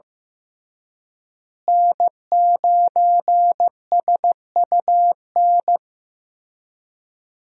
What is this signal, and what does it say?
Morse code, heard as a keyed tone, about 15 wpm, 700 Hz. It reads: N9SUN
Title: USA Call Signs - 5 Characters - Repeated 3x - 15wpm